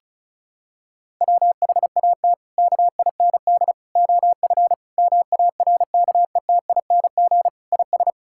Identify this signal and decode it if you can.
Morse code, heard as a keyed tone, about 36 wpm, 700 Hz. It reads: WHAT KIND OF MARKETING IS
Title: Sentences from Top 1000 Words - Review Part 5 - 35wpm